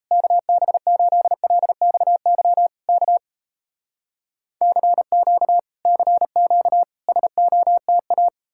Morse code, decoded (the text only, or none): KB8LXY K CQ CQ SOTA